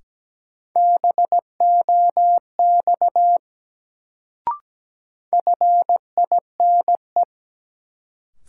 Morse code, heard as a keyed tone, about 17 wpm, 700 Hz. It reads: BOX FINE
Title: Sentences from Top 500 Words - New Word Drill - 17wpm